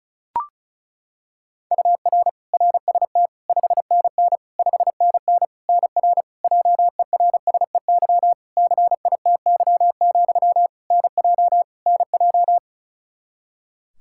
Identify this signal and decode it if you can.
Morse code, heard as a keyed tone, about 34 wpm, 700 Hz. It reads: UR RST 5NN 5NN NR JERSEY CITY, NJ NJ
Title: Practice QSO - Exchange 1 - Beginning - 35wpm